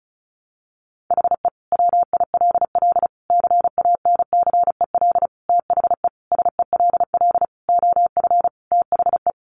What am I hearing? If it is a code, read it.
Morse code, heard as a keyed tone, about 34 wpm, 700 Hz. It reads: HE WILL CANCEL THE SELL OF THE
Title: Sentences from Top 1200 Words 35wpm